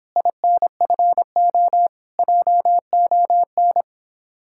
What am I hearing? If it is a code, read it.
Morse code, heard as a keyed tone, about 29 wpm, 700 Hz. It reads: INFO JON